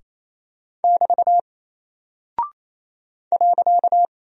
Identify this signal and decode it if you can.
Morse code, heard as a keyed tone, about 27 wpm, 700 Hz. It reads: <BT> .
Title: Single Character (Letter:Number:Punctuation:Prosign) Alt 28wpm